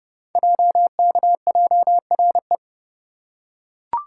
Morse code, heard as a keyed tone, about 30 wpm, 700 Hz. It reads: JKJRE
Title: Code Groups - 5 Letters - 30wpm